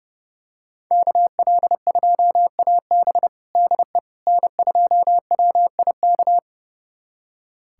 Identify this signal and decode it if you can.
Morse code, heard as a keyed tone, about 30 wpm, 700 Hz. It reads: KL2AB DE N2WIK
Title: USA Callsign DE Callsign - 5 Characters - 30wpm